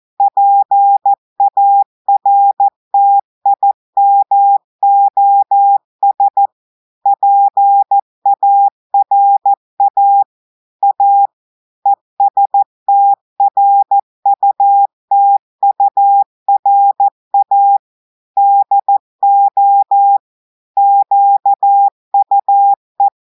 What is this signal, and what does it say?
Morse code, heard as a keyed tone, about 14 wpm, 800 Hz. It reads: PARTIMOS PARA A ESTRUTURA DO QUE
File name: audios / portugais / portugais texte en clair 2 vitesse 14 mots minutes